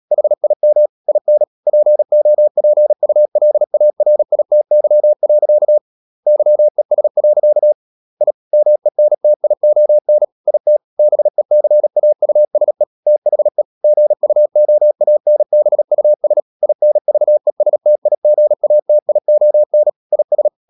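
Morse code, heard as a keyed tone, about 37 wpm, 600 Hz. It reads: HIM IN POPULARITY. YES. I MENTION IT BECAUSE THE GUOANBUS INVESTIGATION IS
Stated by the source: Morse code practice at 35–39 wpm sending English text of Ken Follett's novel Never, from CW med Gnister.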